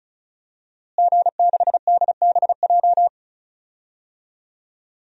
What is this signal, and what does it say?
Morse code, heard as a keyed tone, about 34 wpm, 700 Hz. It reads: G6DBJ
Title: DXCC-100 Call Signs - 5 Characters - Repeated 3x - 35wpm